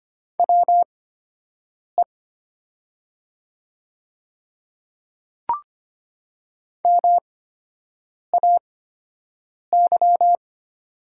Morse code, heard as a keyed tone, about 25 wpm, spaced as for 7 wpm, 700 Hz. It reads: WE MAY
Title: 8x Character Spacing - Top 100 Words - 25wpm